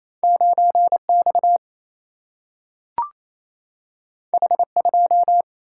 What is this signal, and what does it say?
Morse code, heard as a keyed tone, about 28 wpm, 700 Hz. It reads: 9X H2
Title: Single-Letter Single-Number 28wpm